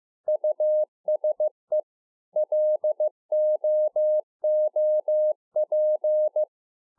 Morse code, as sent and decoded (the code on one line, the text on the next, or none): ..- ... / .-.. --- --- .--.
USE LOOP